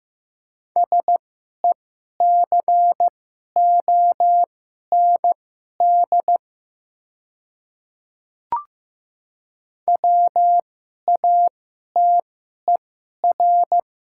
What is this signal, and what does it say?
Morse code, heard as a keyed tone, about 15 wpm, spaced as for 11 wpm, 700 Hz. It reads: SECOND WATER
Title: 2x Character Spacing - Top 300 Words - 15wpm